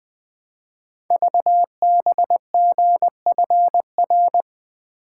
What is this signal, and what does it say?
Morse code, heard as a keyed tone, about 20 wpm, 700 Hz. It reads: VBGFR
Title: Code Groups - 5 Letters - 20wpm